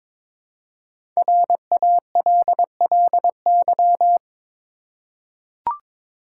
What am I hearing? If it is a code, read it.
Morse code, heard as a keyed tone, about 22 wpm, 700 Hz. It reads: RALLY
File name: 5 Letter Words - First 3 Letters Given - 22wpm